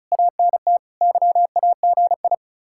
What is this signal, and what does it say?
Morse code, heard as a keyed tone, about 47 wpm, spaced as for 41 wpm, 700 Hz. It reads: ANT YAGI
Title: Speed-Racing Common QSO Phrases - 35wpm